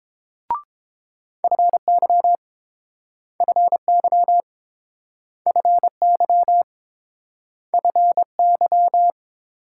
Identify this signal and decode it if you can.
Morse code, heard as a keyed tone, about 29 wpm, 700 Hz. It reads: FY FY FY FY